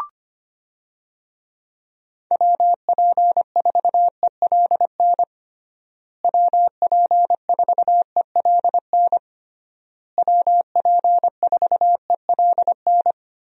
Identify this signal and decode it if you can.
Morse code, heard as a keyed tone, about 25 wpm, 700 Hz. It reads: WP4ELN WP4ELN WP4ELN
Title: USA Call Signs - 6 Characters - Repeated 3x - 25wpm